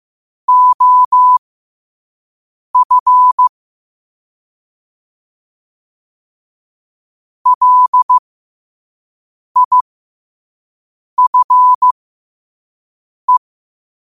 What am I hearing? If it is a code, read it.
Morse code, heard as a keyed tone, about 15 wpm, spaced as for 5 wpm, 1000 Hz. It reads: OF LIFE
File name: 5wpm